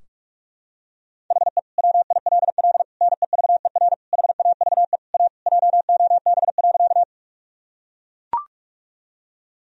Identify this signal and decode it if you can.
Morse code, heard as a keyed tone, about 43 wpm, 700 Hz. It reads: HE WILL NEVER HAVE A JOB.
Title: Sentences from Top 1000 Words - Review Part 3 - 45wpm